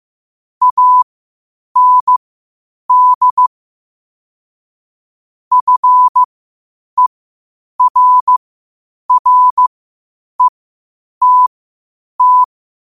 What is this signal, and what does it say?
Morse code, heard as a keyed tone, about 15 wpm, spaced as for 8 wpm, 1000 Hz. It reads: AND FERRETT
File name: qotdfiles / 8wpm